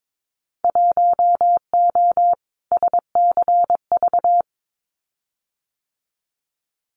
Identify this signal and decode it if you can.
Morse code, heard as a keyed tone, about 22 wpm, 700 Hz. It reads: 1O SCV
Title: Winter Field Day Exchanges 22wpm